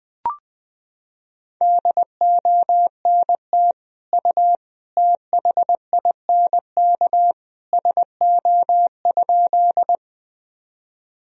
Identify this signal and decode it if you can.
Morse code, heard as a keyed tone, about 20 wpm, 700 Hz. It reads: DONT U THINK SO?
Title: Sentences from Top 300 Words and Abbreviations 20wpm